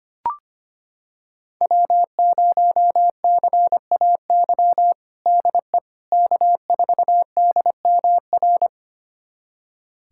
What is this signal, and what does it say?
Morse code, heard as a keyed tone, about 25 wpm, 700 Hz. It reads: W0CAY DE K4DMR
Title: USA Callsign DE Callsign - 5 Characters - 25wpm